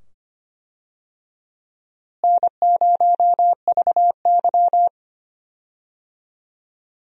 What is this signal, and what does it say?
Morse code, heard as a keyed tone, about 25 wpm, 700 Hz. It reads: N0VY
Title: USA Call Signs - 4 Characters - Repeated 2x - 25wpm